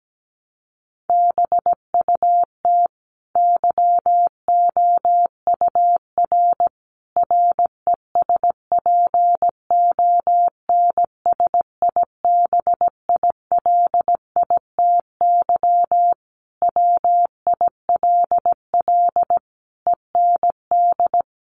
Morse code, heard as a keyed tone, about 17 wpm, 700 Hz. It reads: BUT YOUR RESPONSIBILITY WILL END
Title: Sentences from Top 1200 Words 17wpm